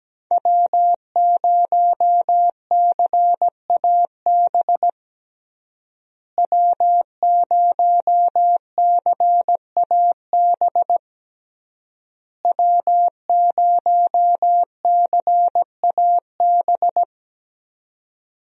Morse code, heard as a keyed tone, about 17 wpm, 700 Hz. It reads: W0CAB W0CAB W0CAB
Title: USA Call Signs - 5 Characters - Repeated 3x - 17wpm